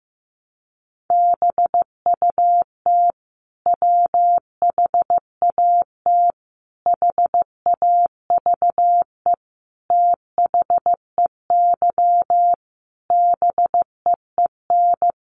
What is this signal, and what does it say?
Morse code, heard as a keyed tone, about 15 wpm, 700 Hz. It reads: BUT WHAT HAVE THEY BEEN
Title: Sentences from Top 100 Words 15wpm